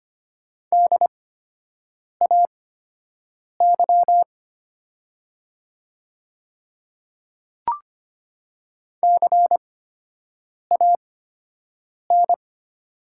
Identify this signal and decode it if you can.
Morse code, heard as a keyed tone, about 25 wpm, spaced as for 7 wpm, 700 Hz. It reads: DAY CAN
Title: 8x Character Spacing - Top 100 Words - 25wpm